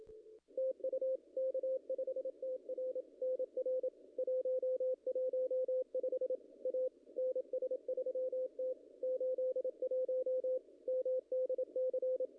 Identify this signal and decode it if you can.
Morse code, heard as a keyed tone, about 27 wpm, 510 Hz, with radio noise. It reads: TU K5TR NR 115 A NS3T 81 MDC
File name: contests / ss-cw-k5tr-op-n5rz -0100